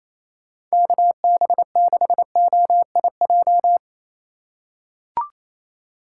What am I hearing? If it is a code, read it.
Morse code, heard as a keyed tone, about 28 wpm, 700 Hz. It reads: KB6OIJ
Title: USA Call Signs - 6 Characters - 3x Character Spacing - 28wpm